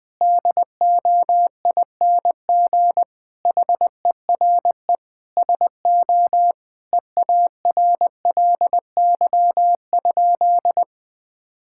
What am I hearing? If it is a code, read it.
Morse code, heard as a keyed tone, about 20 wpm, 700 Hz. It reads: DOING HERE SO EARLY?
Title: Sentences with 5-Word Phrases - Spoken Beginning - 20wpm